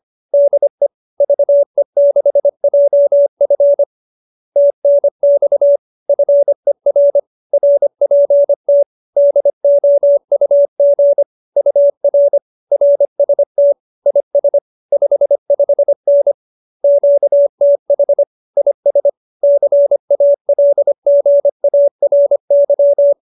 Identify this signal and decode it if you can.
Morse code, heard as a keyed tone, about 25 wpm, 570 Hz. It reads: DE VE6JF TNX FER RPT DOUG UR RST IS 55N QTH IS CALGARY
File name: QSO303 25